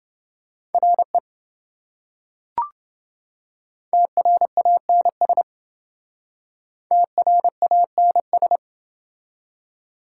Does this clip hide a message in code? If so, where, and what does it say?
Yes: Morse code from 0.7 to 8.6 s, heard as a keyed tone, about 27 wpm, 700 Hz. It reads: RE TRANS TRANS